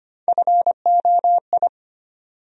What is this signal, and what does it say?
Morse code, heard as a keyed tone, about 25 wpm, 700 Hz. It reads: FOI
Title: USA Call Sign Postfixes - 3 Letters 25wpm